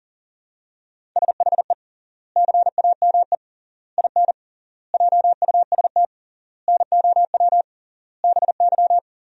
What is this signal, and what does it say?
Morse code, heard as a keyed tone, about 40 wpm, 700 Hz. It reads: SHE CAME IN JUST NOW BY
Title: 3x Word-Spacing - Sentences from Top 300 Words - 40wpm